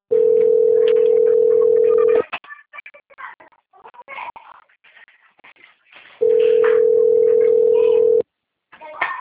Ringback tone (North America), 440 + 480 Hz.
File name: out- - - - - .45544